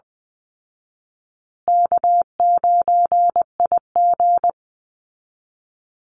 Morse code, heard as a keyed tone, about 20 wpm, 700 Hz. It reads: K9IG
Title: Call Signs Repeated 2x - 20wpm